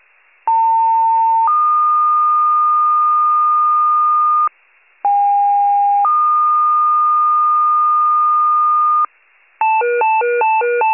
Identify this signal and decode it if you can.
Two-tone page, heard as a one-second tone, then a three-second tone, about 900 then 1220 Hz, about 810 then 1220 Hz. It is heard from 0.5 to 9.1 s.
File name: wcalert